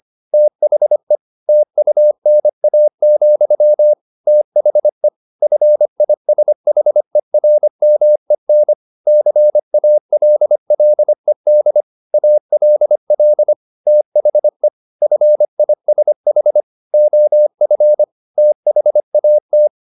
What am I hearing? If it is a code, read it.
Morse code, heard as a keyed tone, about 25 wpm, 600 Hz. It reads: THE TUNA, THE FISHERMEN CALLED ALL THE FISH OF THAT
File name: the old man and the sea part 1 25wpm